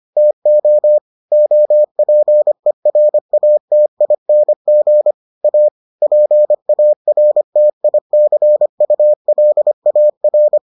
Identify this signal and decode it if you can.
Morse code, heard as a keyed tone, about 25 wpm, 600 Hz. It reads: TO OPERATING A PARTICULAR